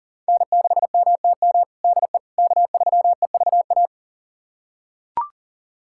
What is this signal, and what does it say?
Morse code, heard as a keyed tone, about 40 wpm, 700 Hz. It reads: N6MTM DE K3EVA
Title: USA Callsign DE Callsign - 5 Characters - 40wpm